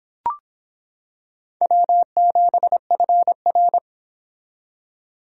Morse code, heard as a keyed tone, about 25 wpm, 700 Hz. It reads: W7FR